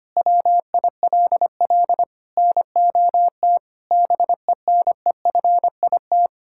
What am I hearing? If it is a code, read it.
Morse code, heard as a keyed tone, about 25 wpm, 700 Hz. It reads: WILL NOT BENEFIT
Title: Sentences from Top 1000 Words - Review Part 6 - 25wpm